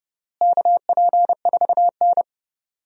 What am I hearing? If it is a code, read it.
Morse code, heard as a keyed tone, about 30 wpm, 700 Hz. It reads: KP4N
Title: DXCC-100 Call Signs - 4 Characters - Repeated 3x - 30wpm